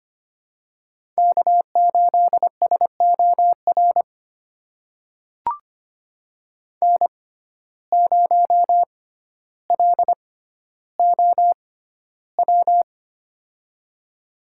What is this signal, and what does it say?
Morse code, heard as a keyed tone, about 25 wpm, spaced as for 13 wpm, 700 Hz. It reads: K8SOR N0LOW
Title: USA Call Signs - 5 Characters - 6x Character Spacing - 25wpm